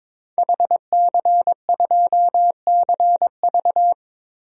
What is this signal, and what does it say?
Morse code, heard as a keyed tone, about 22 wpm, 700 Hz. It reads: HC2CV
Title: DXCC-100 Call Signs - 5 Characters - 22wpm